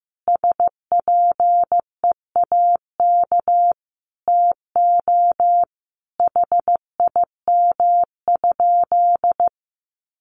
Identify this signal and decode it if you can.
Morse code, heard as a keyed tone, about 15 wpm, 700 Hz. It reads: SPEAK TO HIM?
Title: Sentences from Top 1100 Words 15wpm